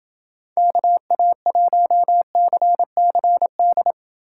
Morse code, heard as a keyed tone, about 29 wpm, 700 Hz. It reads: KA1CCD